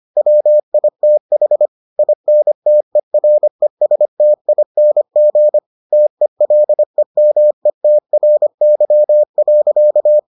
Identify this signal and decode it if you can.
Morse code, heard as a keyed tone, about 25 wpm, 600 Hz. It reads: WITH INTERESTING TELEMETRY.